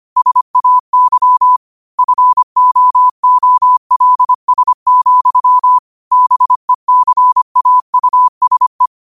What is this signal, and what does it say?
Morse code, heard as a keyed tone, about 25 wpm, 1000 Hz. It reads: SAY FOOLS, BECAUSE
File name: qotdfiles / 25wpm